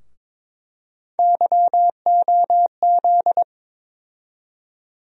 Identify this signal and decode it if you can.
Morse code, heard as a keyed tone, about 22 wpm, 700 Hz. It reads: YOZ
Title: USA Call Sign Postfixes - 3 Letters 22wpm